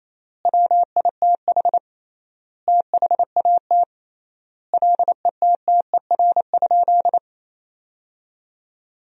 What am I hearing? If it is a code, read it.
Morse code, heard as a keyed tone, about 28 wpm, 700 Hz. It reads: WITH THAT LETTER?
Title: 3x Word-Spacing - Sentences from Top 300 Words - 28wpm